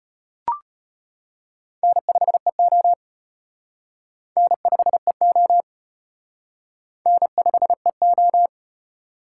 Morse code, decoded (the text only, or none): N5EO N5EO N5EO